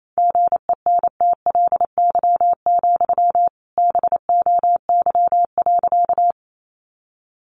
Morse code, heard as a keyed tone, about 28 wpm, 700 Hz. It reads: GENTLY, BOY.
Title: Sentences from Top 1500 Words 28wpm